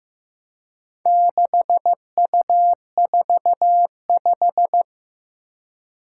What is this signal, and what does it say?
Morse code, heard as a keyed tone, about 15 wpm, 700 Hz. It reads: 6U45
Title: Four Challenging Letter-Number Combinations 15wpm